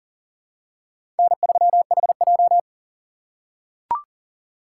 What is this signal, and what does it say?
Morse code, heard as a keyed tone, about 40 wpm, 700 Hz. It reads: N3HJ
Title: USA Call Signs - 4 Characters - 3x Character Spacing - 40wpm